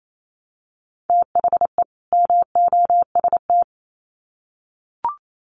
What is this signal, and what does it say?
Morse code, heard as a keyed tone, about 28 wpm, 700 Hz. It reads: THE MOST
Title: Sets of 2 Words 28wpm